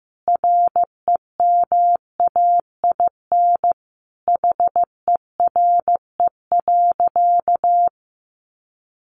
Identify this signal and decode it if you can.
Morse code, heard as a keyed tone, about 15 wpm, 700 Hz. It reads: REMAIN HERE.
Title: Sentences from Top 1200 Words 15wpm